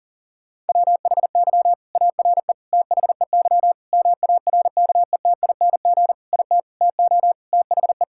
Morse code, heard as a keyed tone, about 40 wpm, 700 Hz. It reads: WHY ARE THEY MARKETING IT TO THE
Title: Sentences from Top 1000 Words - Review Part 5 - 40wpm